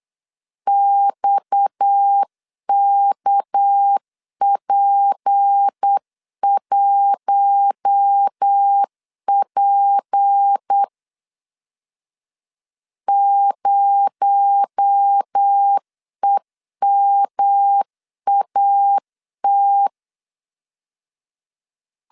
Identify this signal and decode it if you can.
Morse code, heard as a keyed tone, about 8 wpm, 800 Hz. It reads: XKP1P 0EMAT
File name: LEZIONI / 7 GF9Z / 7 GF 8wpm 3108